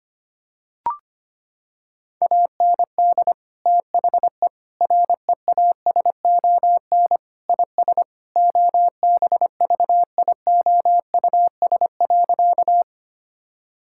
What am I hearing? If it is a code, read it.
Morse code, heard as a keyed tone, about 25 wpm, 700 Hz. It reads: AND THE REASON IS OBVIOUS.
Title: Sentences from Top 1600 Words 25wpm